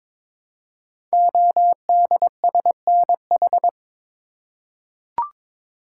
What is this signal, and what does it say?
Morse code, heard as a keyed tone, about 22 wpm, 700 Hz. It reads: ODSNH
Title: Code Groups - 5 Letter:Numbers - 22wpm